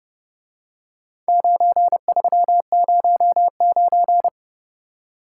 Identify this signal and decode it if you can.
Morse code, heard as a keyed tone, about 30 wpm, 700 Hz. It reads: 9309